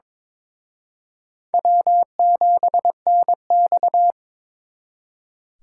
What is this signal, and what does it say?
Morse code, heard as a keyed tone, about 26 wpm, 700 Hz. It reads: W7NX
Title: Speed Racing - USA 1x2 Call Signs 22wpm